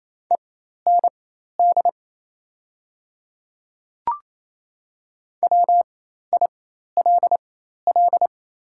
Morse code, heard as a keyed tone, about 28 wpm, spaced as for 13 wpm, 700 Hz. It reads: END WILL
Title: 4x Character Spacing - Top 200 Words - 28wpm